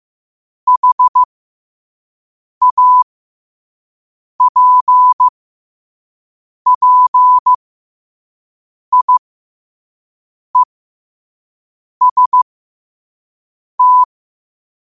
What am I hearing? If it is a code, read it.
Morse code, heard as a keyed tone, about 15 wpm, spaced as for 5 wpm, 1000 Hz. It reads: HAPPIEST
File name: qotdfiles / 5wpm